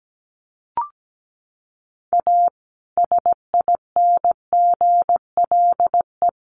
Morse code, heard as a keyed tone, about 17 wpm, 700 Hz. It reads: A SINGLE
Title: Sentences from Top 1100 Words 17wpm